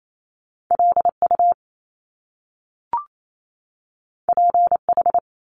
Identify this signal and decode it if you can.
Morse code, heard as a keyed tone, about 28 wpm, 700 Hz. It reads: LU PH
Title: Top Two Letter Combinations in Words 28wpm